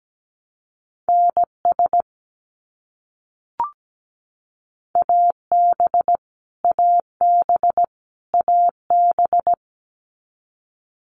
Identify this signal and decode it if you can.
Morse code, heard as a keyed tone, about 17 wpm, 700 Hz. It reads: NS AB AB AB
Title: Canadian Provinces and Territories - Repeated 3x - 17wpm